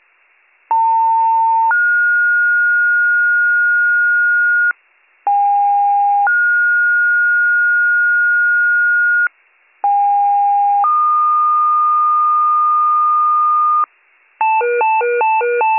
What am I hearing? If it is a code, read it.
Two-tone page, heard as a one-second tone, then a three-second tone, about 900 then 1430 Hz, about 810 then 1430 Hz, about 810 then 1220 Hz, with noise. It is heard from 0.7 to 13.8 s.